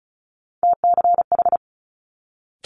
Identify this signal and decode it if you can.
Morse code, heard as a keyed tone, about 36 wpm, 700 Hz. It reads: TCH
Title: Multi-letter Phonemes Quick Fire 35wpm